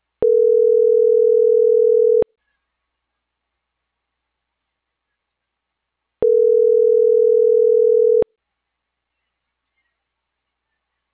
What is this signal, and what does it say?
Ringback tone (North America), 440 + 480 Hz.